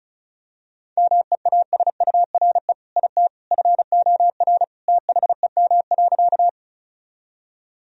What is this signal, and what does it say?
Morse code, heard as a keyed tone, about 36 wpm, 700 Hz. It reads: MEASURE IT FOR THEM.
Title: Sentences from Top 1000 Words - Review Part 2 - 35wpm